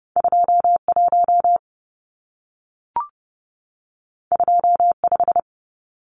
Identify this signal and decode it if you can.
Morse code, heard as a keyed tone, about 30 wpm, 700 Hz. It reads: 21 25
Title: Numbers - 2-Digit 30wpm